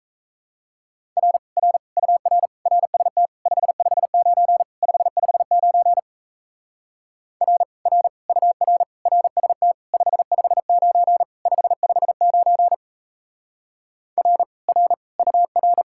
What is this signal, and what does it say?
Morse code, heard as a keyed tone, about 35 wpm, 700 Hz. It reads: R R UR RST 559 559 R R UR RST 559 559 R R UR
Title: Speed-Racing Common QSO Phrases - 28wpm